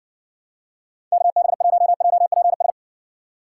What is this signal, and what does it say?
Morse code, heard as a keyed tone, about 60 wpm, 700 Hz. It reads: KB1JPS